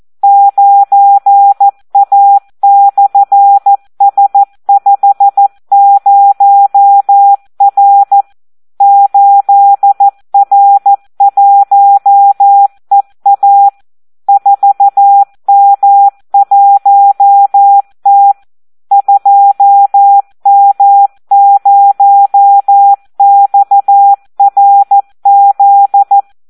Morse code, heard as a keyed tone, about 14 wpm, 800 Hz. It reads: 9A/S50R 8R1EA 4M1T 2M0XRZ
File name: Audio practica CW / SD 1 14